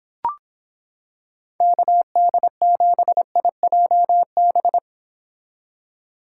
Morse code, heard as a keyed tone, about 26 wpm, 700 Hz. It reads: KD7IJB